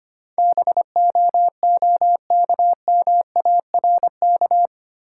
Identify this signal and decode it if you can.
Morse code, heard as a keyed tone, about 25 wpm, 700 Hz. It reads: BOOKMARK
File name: practice / 25wpm / Easy Words with Hints - 25wpm